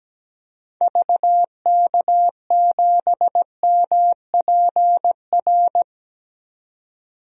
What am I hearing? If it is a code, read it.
Morse code, heard as a keyed tone, about 17 wpm, 700 Hz. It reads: VK7MPR
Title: DXCC-100 Call Signs - 6 Characters - 17wpm